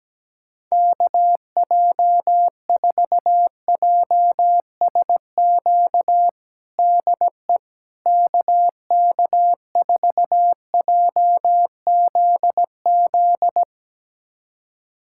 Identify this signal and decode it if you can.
Morse code, heard as a keyed tone, about 17 wpm, 700 Hz. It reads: KJ4JSQ DE KK4JZZ
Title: USA Callsign DE Callsign 17wpm